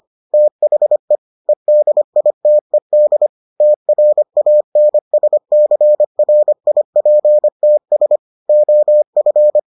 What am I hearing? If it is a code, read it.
Morse code, heard as a keyed tone, about 25 wpm, 600 Hz. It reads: THE EDITED TRANSCRIPTS OF